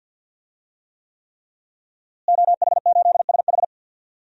Morse code, heard as a keyed tone, about 51 wpm, 700 Hz. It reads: OH8SH
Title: DXCC-100 Call Signs - 5 Characters - Repeated 3x - 50wpm